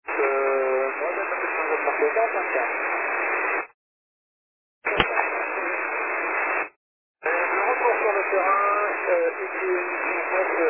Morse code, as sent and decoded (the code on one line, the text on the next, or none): none